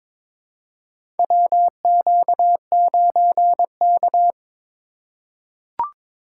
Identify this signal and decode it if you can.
Morse code, heard as a keyed tone, about 22 wpm, 700 Hz. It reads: WQ9K
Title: USA Call Signs - 4 Characters - 4x Character Spacing - 22wpm